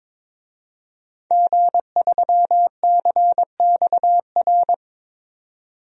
Morse code, heard as a keyed tone, about 22 wpm, 700 Hz. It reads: G3CXR
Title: DXCC-100 Call Signs - 5 Characters - 22wpm